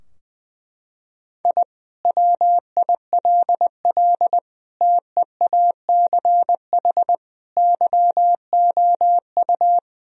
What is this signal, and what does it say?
Morse code, heard as a keyed tone, about 20 wpm, 700 Hz. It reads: I WILL TEACH YOU
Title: Sentences from Top 1000 Words - Review Part 3 - 20wpm